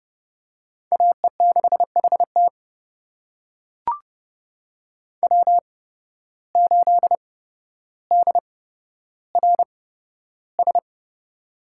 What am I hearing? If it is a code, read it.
Morse code, heard as a keyed tone, about 30 wpm, spaced as for 13 wpm, 700 Hz. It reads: AE6HT W8DRS
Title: USA Call Signs - 5 Characters - 8x Character Spacing - 30wpm